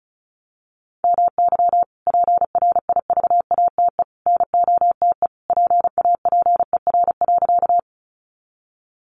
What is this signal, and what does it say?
Morse code, heard as a keyed tone, about 34 wpm, 700 Hz. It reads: MY PRIVATE NOTE PAPER.
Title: Sentences from Top 1000 Words 35wpm